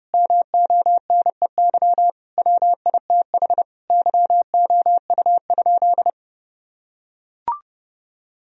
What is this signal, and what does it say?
Morse code, heard as a keyed tone, about 30 wpm, 700 Hz. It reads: MONEY WITH YOU?
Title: Sentences from Top 1000 Words - Review Part 1 - 30wpm